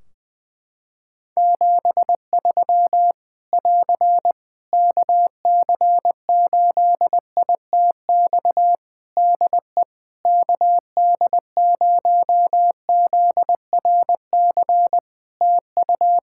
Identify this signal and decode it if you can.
Morse code, heard as a keyed tone, about 20 wpm, 700 Hz. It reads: 73 <AR> KC8ITX DE KD0ZRC TU
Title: Practice QSO - Exchange 4 - Ending - 20wpm